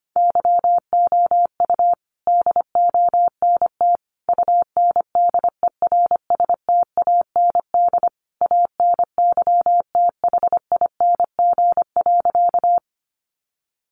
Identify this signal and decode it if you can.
Morse code, heard as a keyed tone, about 25 wpm, 700 Hz. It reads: YOU DONT UNDERSTAND ANYTHING.
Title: Sentences from Top 1000 Words 25wpm